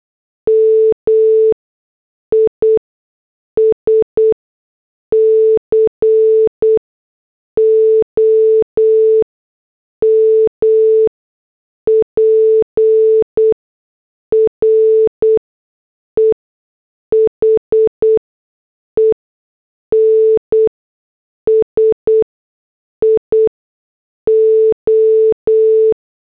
Morse code, heard as a keyed tone, about 8 wpm, spaced as for 6 wpm, 440 Hz. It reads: MISCOMPREHENSIO